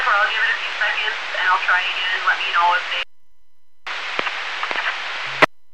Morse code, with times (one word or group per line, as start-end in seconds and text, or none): none